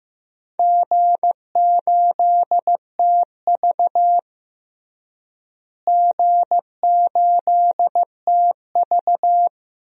Morse code, heard as a keyed tone, about 15 wpm, 700 Hz. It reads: G8TV G8TV
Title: DXCC-100 Call Signs - 4 Characters - Repeated 2x - 15wpm